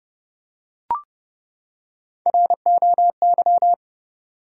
Morse code, heard as a keyed tone, about 30 wpm, 700 Hz. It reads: ROY
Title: USA Common Names - Max 3 Letters - 30wpm